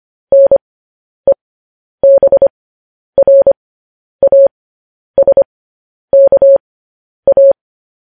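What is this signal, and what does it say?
Morse code, heard as a keyed tone, about 25 wpm, spaced as for 10 wpm, 550 Hz. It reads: NEBRASKA
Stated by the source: Morse code news headlines sent at 10 words per minute.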